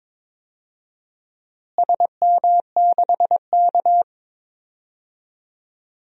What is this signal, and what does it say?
Morse code, heard as a keyed tone, about 22 wpm, 700 Hz. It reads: SM6K
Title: DXCC-100 Call Signs - 4 Characters - Repeated 2x - 22wpm